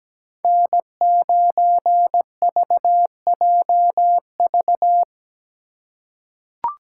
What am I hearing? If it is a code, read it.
Morse code, heard as a keyed tone, about 17 wpm, 700 Hz. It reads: N9VJV
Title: USA Call Signs - 5 Characters - 8x Character Spacing - 17wpm